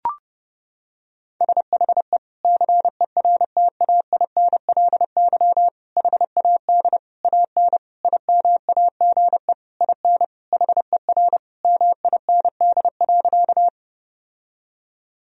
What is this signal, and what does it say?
Morse code, heard as a keyed tone, about 30 wpm, 700 Hz. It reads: SHE CERTAINLY HAD AN IMAGE IN HER MIND.
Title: Sentences from Top 1000 Words - Review Part 4 - 30wpm